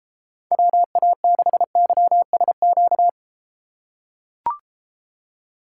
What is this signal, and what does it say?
Morse code, heard as a keyed tone, about 32 wpm, 700 Hz. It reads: WA6YSQ